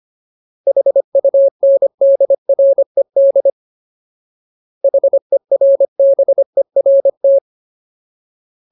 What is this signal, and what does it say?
Morse code, heard as a keyed tone, about 25 wpm, 550 Hz. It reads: HUNDRED HERBERT
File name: Ultimate Word List / 25WPM Ultimate